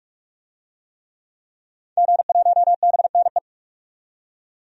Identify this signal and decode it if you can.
Morse code, heard as a keyed tone, about 42 wpm, 700 Hz. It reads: G1BNE
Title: DXCC-100 Call Signs - 5 Characters - 45wpm